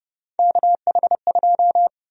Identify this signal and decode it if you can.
Morse code, heard as a keyed tone, about 29 wpm, 700 Hz. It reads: KH2
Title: USA Call Sign Prefixes - 30wpm